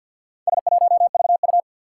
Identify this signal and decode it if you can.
Morse code, heard as a keyed tone, about 47 wpm, 700 Hz. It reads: S1VU
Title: Four Challenging Letter-Number Combinations 50wpm